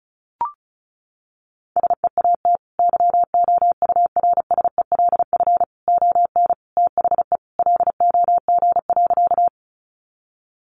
Morse code, heard as a keyed tone, about 34 wpm, 700 Hz. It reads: SEAT YOURSELF ON THE LOG.
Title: Sentences from Top 1500 Words 35wpm